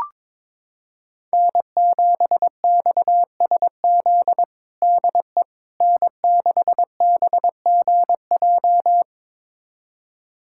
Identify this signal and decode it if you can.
Morse code, heard as a keyed tone, about 21 wpm, 700 Hz. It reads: N7XSZ DE N6BGJ
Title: USA Callsign DE Callsign - 5 Characters - 22wpm